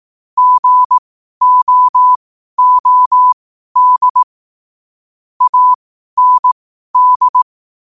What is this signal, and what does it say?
Morse code, heard as a keyed tone, about 18 wpm, spaced as for 12 wpm, 1000 Hz. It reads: GOOD AND
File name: qotdfiles / 12wpm